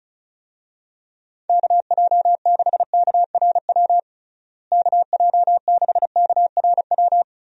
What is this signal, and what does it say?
Morse code, heard as a keyed tone, about 34 wpm, 700 Hz. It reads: KJ6KRW KJ6KRW
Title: USA Call Signs - 6 Characters - Repeated 3x - 35wpm